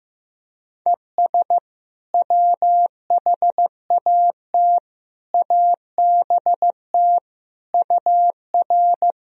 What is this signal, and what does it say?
Morse code, heard as a keyed tone, about 15 wpm, 700 Hz. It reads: ES WHAT ABT UR
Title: Sentences from Top 200 Words and Abbreviations 15wpm